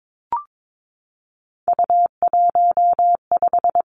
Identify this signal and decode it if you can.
Morse code, heard as a keyed tone, about 22 wpm, 700 Hz. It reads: U15
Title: Three Challenging Letter-Number Combinations 22wpm